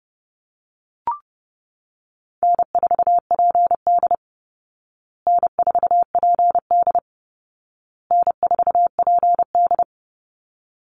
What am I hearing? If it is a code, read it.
Morse code, heard as a keyed tone, about 30 wpm, 700 Hz. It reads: N4PD N4PD N4PD
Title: Call Signs Repeated 2x - 30wpm